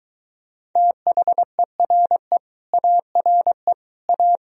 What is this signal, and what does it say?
Morse code, heard as a keyed tone, about 23 wpm, 700 Hz. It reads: THERE ARE A